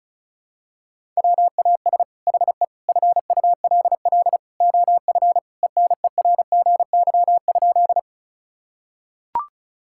Morse code, heard as a keyed tone, about 36 wpm, 700 Hz. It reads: WAS HE FULL OF ENERGY?
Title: Sentences from Top 1000 Words - Review Part 6 - 35wpm